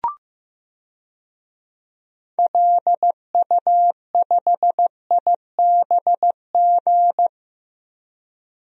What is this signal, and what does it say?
Morse code, heard as a keyed tone, about 15 wpm, 700 Hz. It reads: LU5IBG
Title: DXCC-100 Call Signs - 6 Characters - 15wpm